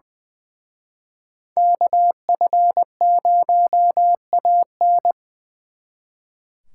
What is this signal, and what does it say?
Morse code, heard as a keyed tone, about 24 wpm, 700 Hz. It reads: KF0AN
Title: Speed Racing - USA 2x2 Call Signs 20wpm